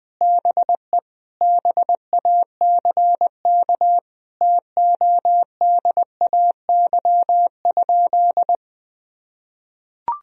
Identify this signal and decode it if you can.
Morse code, heard as a keyed tone, about 20 wpm, 700 Hz. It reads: BE BACK TODAY?
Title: Sentences from Top 1000 Words - Review Part 4 - 20wpm